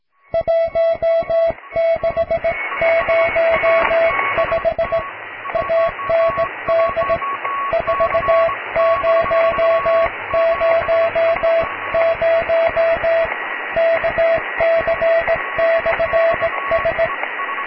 Morse code, heard as a keyed tone, about 17 wpm, 660 Hz, with radio noise. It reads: 1605 AND 4000 KC/S
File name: cw / tekst(b)+QRM